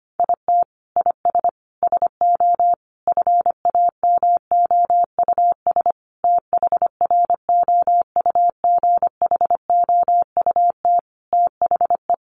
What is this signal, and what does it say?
Morse code, heard as a keyed tone, about 25 wpm, 700 Hz. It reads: IT IS SO FAMOUS THROUGHOUT THE
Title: Sentences from Top 1500 Words 25wpm